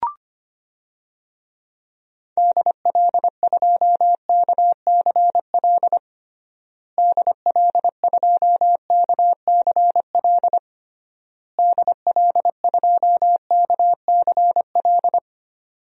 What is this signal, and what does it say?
Morse code, heard as a keyed tone, about 25 wpm, 700 Hz. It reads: DL2KCL DL2KCL DL2KCL
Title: DXCC-100 Call Signs - 6 Characters - Repeated 3x - 25wpm